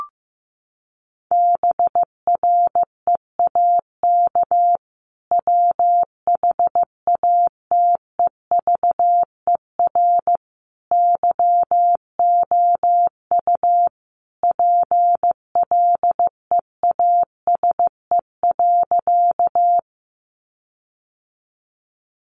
Morse code, heard as a keyed tone, about 14 wpm, 700 Hz. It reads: BREAK WHATEVER YOU PLEASE.
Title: Sentences from Top 1500 Words 15wpm